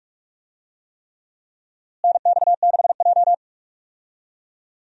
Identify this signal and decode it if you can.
Morse code, heard as a keyed tone, about 46 wpm, 700 Hz. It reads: NX6J
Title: USA Call Signs - 4 Characters - Repeated 3x - 45wpm